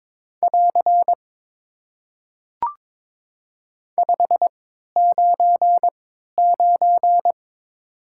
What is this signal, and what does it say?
Morse code, heard as a keyed tone, about 22 wpm, spaced as for 13 wpm, 700 Hz. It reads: <AR> 599
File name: Top 90 QSO Elements - 3x Character Spacing - 22wpm